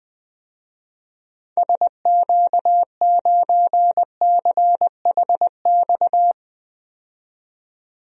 Morse code, heard as a keyed tone, about 21 wpm, 700 Hz. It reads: SQ9CHX
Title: DXCC-100 Call Signs - 6 Characters - 20wpm